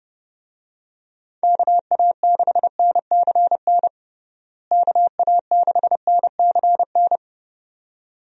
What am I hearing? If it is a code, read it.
Morse code, heard as a keyed tone, about 32 wpm, 700 Hz. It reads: KA6NCN KA6NCN
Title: USA Call Signs - 6 Characters - Repeated 2x - 30wpm